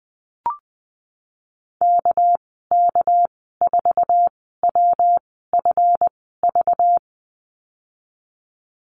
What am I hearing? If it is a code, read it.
Morse code, heard as a keyed tone, about 19 wpm, spaced as for 14 wpm, 700 Hz. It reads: KK4WFV
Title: 219 Lesson 35g - Number 0 - Call Signs 20wpm